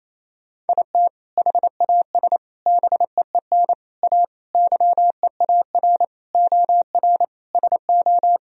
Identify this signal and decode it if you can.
Morse code, heard as a keyed tone, about 28 wpm, 700 Hz. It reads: IT HAS BEEN A YEAR OR SO
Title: Sentences from Top 1000 Words - Review Part 1 - 28wpm